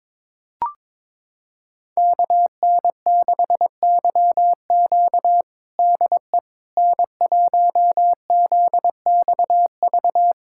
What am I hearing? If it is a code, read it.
Morse code, heard as a keyed tone, about 21 wpm, 700 Hz. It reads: KN6YQ DE N1ZXV
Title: USA Callsign DE Callsign - 5 Characters - 22wpm